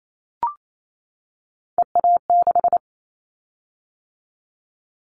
Morse code, heard as a keyed tone, about 28 wpm, 700 Hz. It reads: EA6
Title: Callsign Prefixes for Top 50 DXCC - 28wpm